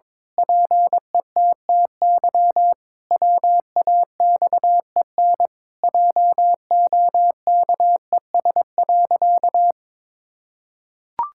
Morse code, heard as a keyed tone, about 22 wpm, 700 Hz. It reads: PETTY WAXEN JOKES.
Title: Pangram Challenge v2 - 22wpm